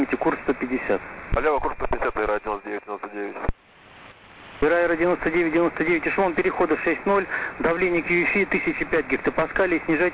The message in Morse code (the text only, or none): none